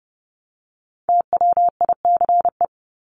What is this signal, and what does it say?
Morse code, heard as a keyed tone, about 30 wpm, 700 Hz. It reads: TWICE
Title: Top 1501-1750 Words 30wpm